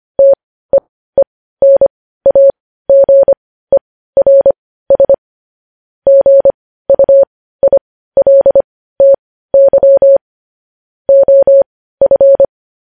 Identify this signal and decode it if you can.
Morse code, heard as a keyed tone, about 25 wpm, spaced as for 15 wpm, 550 Hz. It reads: TEENAGERS GUILTY OF
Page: news-headlines-in-morse-code-at-15-wpm